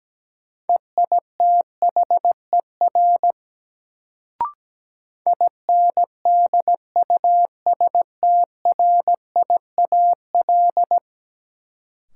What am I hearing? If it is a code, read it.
Morse code, heard as a keyed tone, about 17 wpm, 700 Hz. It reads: EITHER INDUSTRIAL
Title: Sets of 5 Words - New Word Drill - 17wpm